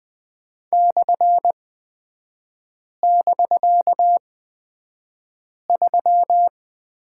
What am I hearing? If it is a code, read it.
Morse code, heard as a keyed tone, about 20 wpm, spaced as for 5 wpm, 700 Hz. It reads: /<BK>3
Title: 250 Lesson 40e - BK - Three Characters 20wpm